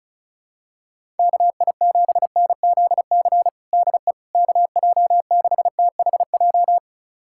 Morse code, heard as a keyed tone, about 36 wpm, 700 Hz. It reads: KI7NZC DE KJ6THJ
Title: USA Callsign DE Callsign - 6 Characters - 35wpm